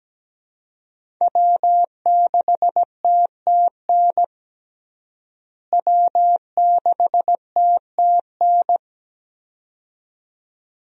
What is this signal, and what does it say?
Morse code, heard as a keyed tone, about 17 wpm, 700 Hz. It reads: W6TTN W6TTN
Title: USA Call Signs - 5 Characters - Repeated 2x - 17wpm